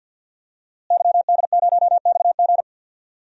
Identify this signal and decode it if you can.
Morse code, heard as a keyed tone, about 48 wpm, 700 Hz. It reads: YD0XG
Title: DXCC-100 Call Signs - 5 Characters - Repeated 2x - 50wpm